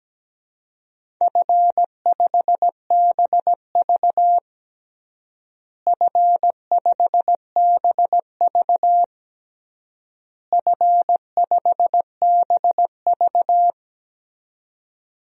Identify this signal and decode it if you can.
Morse code, heard as a keyed tone, about 17 wpm, 700 Hz. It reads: F5BV F5BV F5BV
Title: DXCC-100 Call Signs - 4 Characters - Repeated 3x - 17wpm